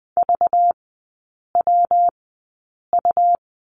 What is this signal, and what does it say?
Morse code, heard as a keyed tone, about 20 wpm, spaced as for 8 wpm, 690 Hz. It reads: VWU
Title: 191 Lesson 31e - Letter v - Three Characters 20wpm